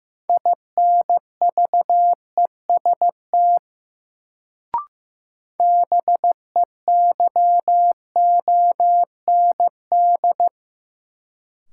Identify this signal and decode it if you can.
Morse code, heard as a keyed tone, about 15 wpm, 700 Hz. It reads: INVEST BEYOND
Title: Sentences from Top 1100 Words - New Word Drill Alt - 15wpm